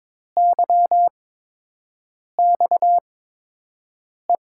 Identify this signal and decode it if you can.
Morse code, heard as a keyed tone, about 21 wpm, spaced as for 6 wpm, 700 Hz. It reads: YXE
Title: USA Call Signs - 6 Characters - 8x Character Spacing - 22wpm